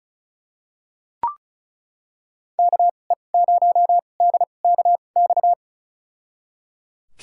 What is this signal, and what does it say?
Morse code, heard as a keyed tone, about 36 wpm, spaced as for 25 wpm, 700 Hz. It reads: KE0DKX
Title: USA Call Signs - 6 Characters - 2x Character Spacing - 35wpm